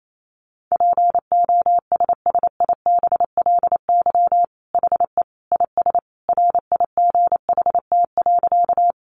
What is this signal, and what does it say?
Morse code, heard as a keyed tone, about 28 wpm, 700 Hz. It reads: POSSIBLY HE IS RIGHT.
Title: Sentences from Top 1300 Words 28wpm